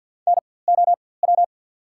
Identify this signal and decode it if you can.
Morse code, heard as a keyed tone, about 54 wpm, spaced as for 24 wpm, 700 Hz. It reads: NOW
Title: 4x Character Spacing - Top 300 Words - 50wpm